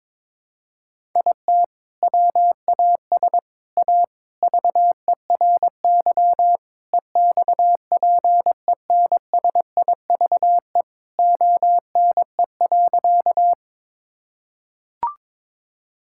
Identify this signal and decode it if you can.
Morse code, heard as a keyed tone, about 22 wpm, 700 Hz. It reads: IT WAS A VERY EXPENSIVE ONE.
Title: Sentences from Top 1000 Words - Review Part 6 - 22wpm